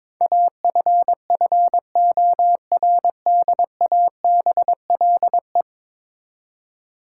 Morse code, heard as a keyed tone, about 22 wpm, 700 Hz. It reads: AFFORDABLE
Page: morse-code-ninja-practice-22wpm